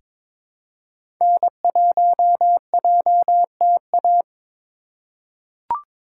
Morse code, heard as a keyed tone, about 22 wpm, 700 Hz. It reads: N1JTA
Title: USA Call Signs - 5 Characters - 2x Character Spacing - 22wpm